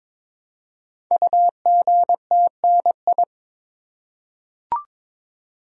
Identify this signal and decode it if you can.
Morse code, heard as a keyed tone, about 22 wpm, 700 Hz. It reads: UGTNI
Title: Code Groups - 5 Letters - 22wpm